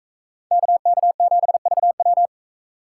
Morse code, heard as a keyed tone, about 41 wpm, 700 Hz. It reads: KK7VW